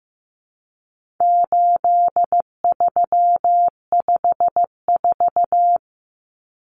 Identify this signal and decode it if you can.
Morse code, heard as a keyed tone, about 15 wpm, 700 Hz. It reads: 8354
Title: Numbers - 4-Digit 15wpm